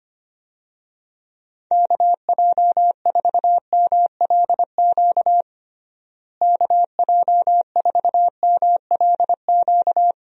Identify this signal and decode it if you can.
Morse code, heard as a keyed tone, about 25 wpm, 700 Hz. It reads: KJ4MLQ KJ4MLQ
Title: USA Call Signs - 6 Characters - Repeated 3x - 25wpm